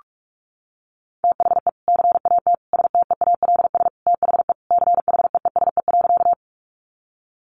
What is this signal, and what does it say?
Morse code, heard as a keyed tone, about 44 wpm, 700 Hz. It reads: THE CAT STEALS THE CHEESE.
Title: Easy Phrases 45wpm